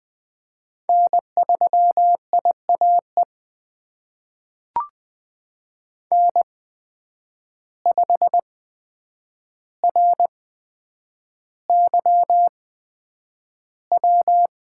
Morse code, heard as a keyed tone, about 20 wpm, spaced as for 9 wpm, 700 Hz. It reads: N3IAE N5RYW
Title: USA Call Signs - 5 Characters - 8x Character Spacing - 20wpm